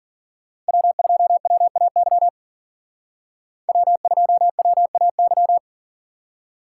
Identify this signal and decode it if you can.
Morse code, heard as a keyed tone, about 52 wpm, 700 Hz. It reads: W2WAY W2WAY